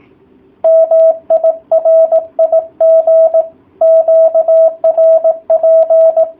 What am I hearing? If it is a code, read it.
Morse code, heard as a keyed tone, about 18 wpm, 650 Hz. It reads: MI RIG QRP